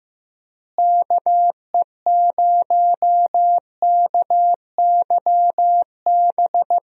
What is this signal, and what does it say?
Morse code, heard as a keyed tone, about 16 wpm, 700 Hz. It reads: KE0KYB
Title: USA Call Signs - 6 Characters - Repeated 3x - 15wpm